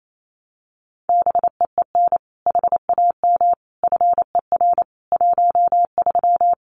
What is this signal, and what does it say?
Morse code, heard as a keyed tone, about 28 wpm, 700 Hz. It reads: BEEN HAM FER 13
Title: Common QSO Phrases 28wpm